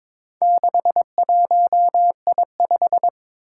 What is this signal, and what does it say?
Morse code, heard as a keyed tone, about 23 wpm, 700 Hz. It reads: 61I5
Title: Four Challenging Letter-Number Combinations 22wpm